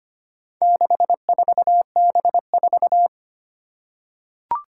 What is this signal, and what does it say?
Morse code, heard as a keyed tone, about 25 wpm, 700 Hz. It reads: 64B4
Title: Four Challenging Letter-Number Combinations 25wpm